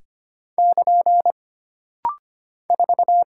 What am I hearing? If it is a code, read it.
Morse code, heard as a keyed tone, about 25 wpm, 700 Hz. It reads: <KN> 4
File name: Single Character (Letter:Number:Punctuation:Prosign) - Rapid-Fire - 25wpm